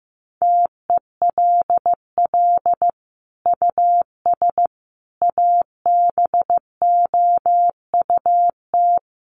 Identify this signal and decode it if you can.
Morse code, heard as a keyed tone, about 15 wpm, 700 Hz. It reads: TELL US ABOUT
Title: Sentences from Top 1200 Words 15wpm